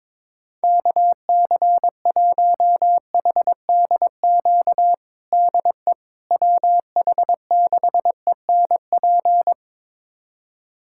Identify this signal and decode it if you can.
Morse code, heard as a keyed tone, about 22 wpm, 700 Hz. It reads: KC1HDQ DE WH6ENP
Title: USA Callsign DE Callsign - 6 Characters - 22wpm